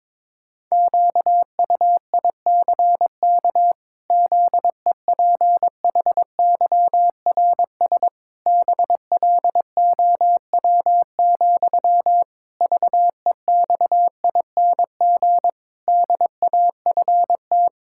Morse code, heard as a keyed tone, about 21 wpm, 700 Hz. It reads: QUICK ZEPHYRS BLOW, VEXING DAFT
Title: Pangram Challenge v2 - 22wpm